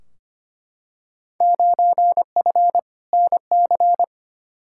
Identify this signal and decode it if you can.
Morse code, heard as a keyed tone, about 25 wpm, 700 Hz. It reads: 9F NC
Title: Field Day Exchanges Encore - 25wpm